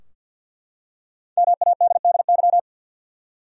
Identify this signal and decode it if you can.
Morse code, heard as a keyed tone, about 52 wpm, 700 Hz. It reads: MADDY
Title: USA Common Names - Max 5 Letters - 50wpm